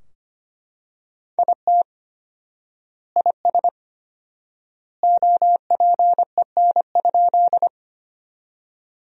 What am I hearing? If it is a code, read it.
Morse code, heard as a keyed tone, about 25 wpm, 700 Hz. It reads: IT IS OPEN?
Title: 4x Word-Spacing - Sentences from Top 300 Words - 25wpm